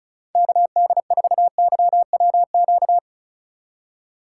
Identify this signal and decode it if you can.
Morse code, heard as a keyed tone, about 36 wpm, 690 Hz. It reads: KD4YWQ
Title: USA Call Signs - 6 Characters - 2x Character Spacing - 35wpm